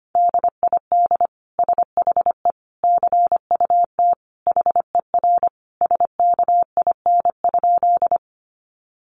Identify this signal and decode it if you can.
Morse code, heard as a keyed tone, about 25 wpm, 700 Hz. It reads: DID SHE CUT HER SKIN?
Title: Sentences from Top 1300 Words 25wpm